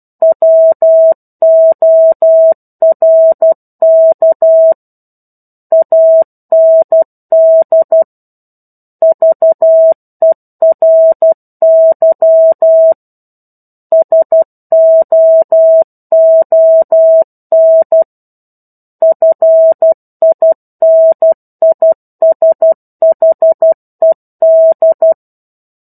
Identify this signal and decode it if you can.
Morse code, heard as a keyed tone, about 12 wpm, 650 Hz. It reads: WORK AND VERY SOON FINISHED
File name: 01-Chapter-01 12 WPM